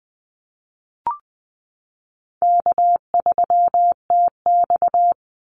Morse code, heard as a keyed tone, about 20 wpm, 700 Hz. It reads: K3TX
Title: Easy Call Signs Repeated 2x - 20wpm